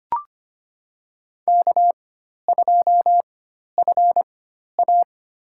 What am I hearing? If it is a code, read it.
Morse code, heard as a keyed tone, about 25 wpm, spaced as for 12 wpm, 690 Hz. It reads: K2FA
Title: USA Call Signs - 4 Characters - 4x Character Spacing - 25wpm